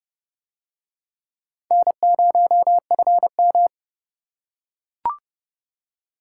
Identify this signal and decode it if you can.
Morse code, heard as a keyed tone, about 29 wpm, 700 Hz. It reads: N0FM